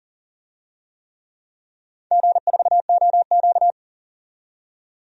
Morse code, heard as a keyed tone, about 38 wpm, 700 Hz. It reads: G4OQ
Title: DXCC-100 Call Signs - 4 Characters - 40wpm